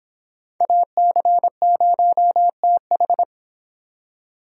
Morse code, heard as a keyed tone, about 26 wpm, 700 Hz. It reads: AC0TH